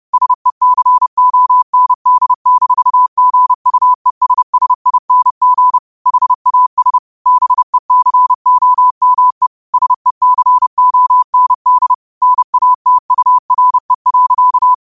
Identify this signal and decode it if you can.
Morse code, heard as a keyed tone, about 30 wpm, 1000 Hz. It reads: SECOND-GUESSING HAS BECOME SECOND NATURE.
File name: qotdfiles / 30wpm